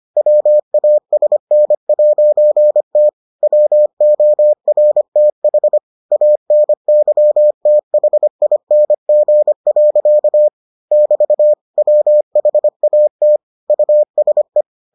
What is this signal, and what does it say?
Morse code, heard as a keyed tone, about 25 wpm, 600 Hz. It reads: WASN'T WORTH ANYTHING. <BT> WHAT USE